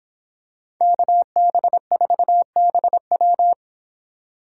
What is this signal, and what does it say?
Morse code, heard as a keyed tone, about 25 wpm, 700 Hz. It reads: KB4BW